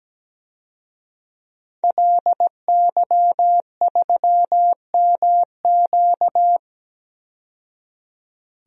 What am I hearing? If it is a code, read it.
Morse code, heard as a keyed tone, about 17 wpm, 700 Hz. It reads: LY3MQ
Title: DXCC-100 Call Signs - 5 Characters - 17wpm